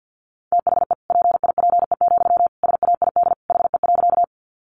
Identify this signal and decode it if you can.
Morse code, heard as a keyed tone, about 50 wpm, 700 Hz. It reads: THE PIPE, SAID HE.
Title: Sentences from Top 1500 Words 50wpm